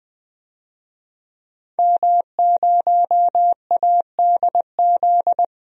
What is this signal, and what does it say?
Morse code, heard as a keyed tone, about 21 wpm, 700 Hz. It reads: M0ADZ
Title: DXCC-100 Call Signs - 5 Characters - Repeated 3x - 20wpm